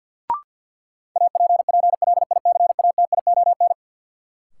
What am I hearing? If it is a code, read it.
Morse code, heard as a keyed tone, about 47 wpm, 700 Hz. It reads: APPLICATION
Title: Sentences from Top 1200 Words - New Word Drill Alt - 50wpm